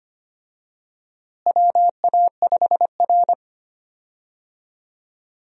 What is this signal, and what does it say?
Morse code, heard as a keyed tone, about 25 wpm, 700 Hz. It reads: WA5R
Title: USA Call Signs - 4 Characters - Repeated 2x - 25wpm